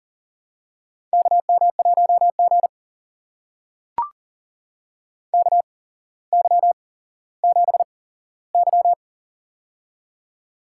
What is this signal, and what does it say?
Morse code, heard as a keyed tone, about 39 wpm, spaced as for 17 wpm, 700 Hz. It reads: KM1G KY7Y
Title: USA Call Signs - 4 Characters - 8x Character Spacing - 40wpm